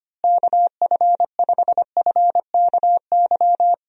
Morse code, heard as a keyed tone, about 25 wpm, 700 Hz. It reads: KF5FKY
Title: USA Call Signs - 6 Characters - Repeated 2x - 25wpm